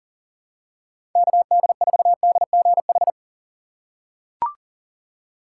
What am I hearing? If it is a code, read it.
Morse code, heard as a keyed tone, about 40 wpm, 700 Hz. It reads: KD4DGH
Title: USA Call Signs - 6 Characters - 6x Character Spacing - 40wpm